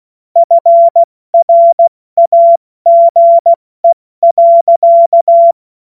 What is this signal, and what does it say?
Morse code, heard as a keyed tone, about 16 wpm, 680 Hz. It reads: FRAGE.
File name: Witz-26-August-2025-70-F80